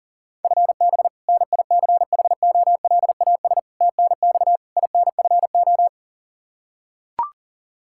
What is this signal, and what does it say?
Morse code, heard as a keyed tone, about 40 wpm, 700 Hz. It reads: FB NICHOLAS TNX INFO
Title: K1USN SST QSO Practice 40wpm